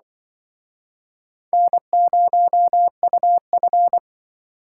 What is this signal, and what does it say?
Morse code, heard as a keyed tone, about 24 wpm, 700 Hz. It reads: N0UF
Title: Speed Racing - USA 1x2 Call Signs 20wpm